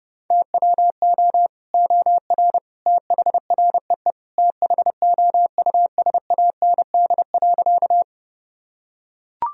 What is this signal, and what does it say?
Morse code, heard as a keyed tone, about 30 wpm, 700 Hz. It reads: TWO OR THREE THOUSAND.
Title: Sentences from Top 1000 Words - Review Part 3 - 30wpm